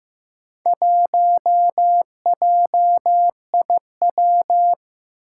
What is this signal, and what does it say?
Morse code, heard as a keyed tone, about 15 wpm, 700 Hz. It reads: 1JIW
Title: Four Challenging Letter-Number Combinations 15wpm